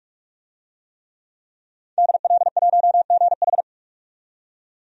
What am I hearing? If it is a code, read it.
Morse code, heard as a keyed tone, about 45 wpm, 700 Hz. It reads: DL1GH
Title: DXCC-100 Call Signs - 5 Characters - 45wpm